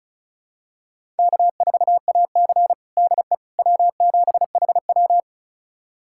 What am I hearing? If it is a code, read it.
Morse code, heard as a keyed tone, about 36 wpm, 700 Hz. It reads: K4AC DE W7HW
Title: USA Callsign DE Callsign - 4 Characters - 35wpm